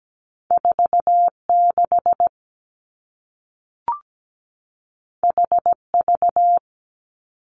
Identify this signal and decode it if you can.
Morse code, heard as a keyed tone, about 17 wpm, 700 Hz. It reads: 4 6 H V
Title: Two Challenging Letter-Number Combinations 17wpm